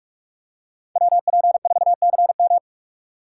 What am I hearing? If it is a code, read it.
Morse code, heard as a keyed tone, about 44 wpm, 700 Hz. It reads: WP4CM
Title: USA Call Signs - 5 Characters - 4x Character Spacing - 45wpm